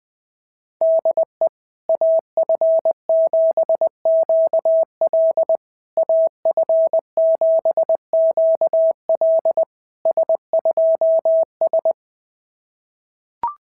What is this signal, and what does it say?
Morse code, heard as a keyed tone, about 20 wpm, 660 Hz. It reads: DE AF7QL AF7QL S2S
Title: SOTA QSO Practice 20wpm